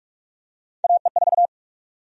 Morse code, heard as a keyed tone, about 47 wpm, 700 Hz. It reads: AE4
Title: USA Call Sign Prefixes - Repeated 2x - 45wpm